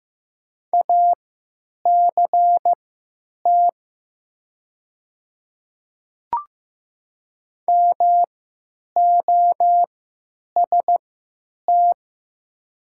Morse code, heard as a keyed tone, about 15 wpm, spaced as for 9 wpm, 700 Hz. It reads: ACT MOST
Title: 3x Character Spacing - Top 300 Words - 15wpm